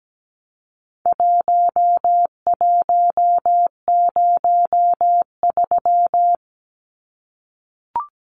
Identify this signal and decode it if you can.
Morse code, heard as a keyed tone, about 17 wpm, 700 Hz. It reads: 1103
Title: Numbers - Contest Style - 4-Digit 17wpm